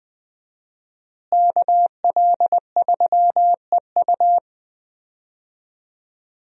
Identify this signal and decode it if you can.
Morse code, heard as a keyed tone, about 21 wpm, 700 Hz. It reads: KL3EU
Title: USA Call Signs - 5 Characters - Repeated 3x - 20wpm